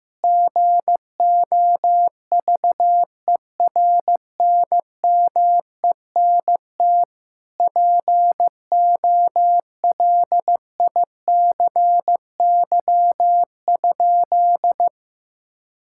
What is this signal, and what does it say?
Morse code, heard as a keyed tone, about 15 wpm, 700 Hz. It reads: GOVERNMENT POLICY?
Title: Sentences from Top 1000 Words - Review Part 5 - 15wpm